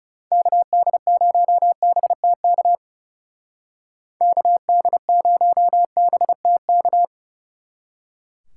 Morse code, heard as a keyed tone, about 39 wpm, 700 Hz. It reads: KD0BTK KD0BTK